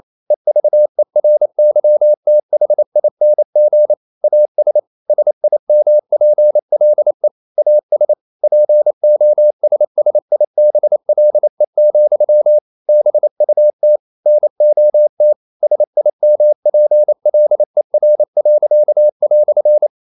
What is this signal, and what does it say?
Morse code, heard as a keyed tone, about 28 wpm, 600 Hz. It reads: EVERYTHING AS SIMPLE AS POSSIBLE, BUT NOT SIMPLER."
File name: CW / Practice / EnglishQuotations0228